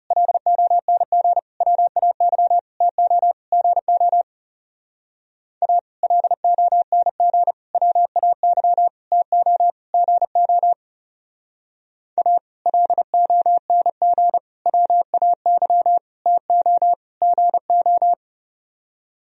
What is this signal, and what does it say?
Morse code, heard as a keyed tone, about 37 wpm, 700 Hz. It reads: LONG WAY TO GO A LONG WAY TO GO A LONG WAY TO GO
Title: Speed-Racing - Sets of 5 Words 30wpm